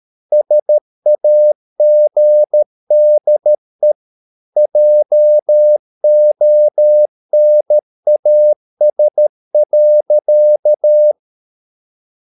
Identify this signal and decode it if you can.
Morse code, heard as a keyed tone, about 13 wpm, 600 Hz. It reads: SAGDE JONAS.
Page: Ferie 13 wpm